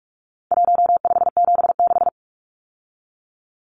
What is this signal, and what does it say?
Morse code, heard as a keyed tone, about 44 wpm, 700 Hz. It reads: 1576
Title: Numbers - 4-Digit 45wpm